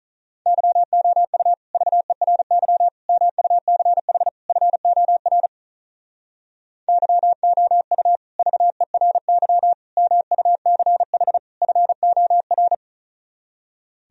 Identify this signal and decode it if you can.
Morse code, heard as a keyed tone, about 46 wpm, 700 Hz. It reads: YOU VERY MUCH FOR YOU VERY MUCH FOR